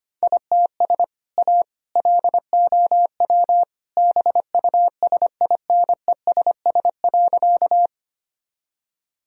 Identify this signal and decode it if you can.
Morse code, heard as a keyed tone, about 25 wpm, 700 Hz. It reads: ITS A LOW BUSINESS.
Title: Sentences from Top 1000 Words - Review Part 1 - 25wpm